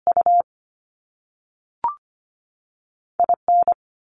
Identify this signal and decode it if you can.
Morse code, heard as a keyed tone, about 25 wpm, 700 Hz. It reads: U I N
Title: Top Two Letter Combinations in Words 25wpm